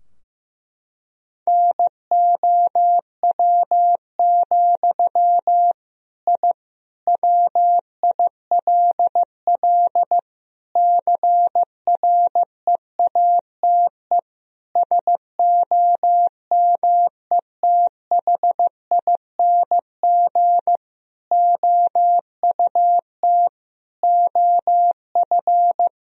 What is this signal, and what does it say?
Morse code, heard as a keyed tone, about 15 wpm, 700 Hz. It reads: NOW, I WILL CREATE SOMETHING OUT OF
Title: Sentences from Top 1000 Words - Review Part 3 - 15wpm